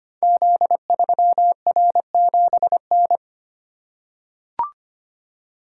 Morse code, heard as a keyed tone, about 25 wpm, 700 Hz. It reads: Z3R7N
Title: Code Groups - 5 Letter:Numbers - 25wpm